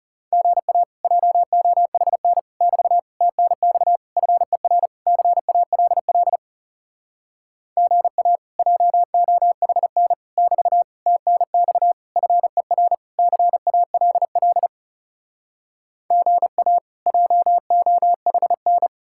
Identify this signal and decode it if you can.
Morse code, heard as a keyed tone, about 40 wpm, 700 Hz. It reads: GA JOHN <BT> TNX FER CALL GA JOHN <BT> TNX FER CALL GA JOHN